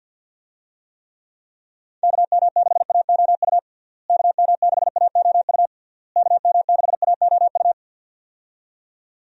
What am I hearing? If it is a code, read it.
Morse code, heard as a keyed tone, about 51 wpm, 700 Hz. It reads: KM6AOU KM6AOU KM6AOU
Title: USA Call Signs - 6 Characters - Repeated 3x - 50wpm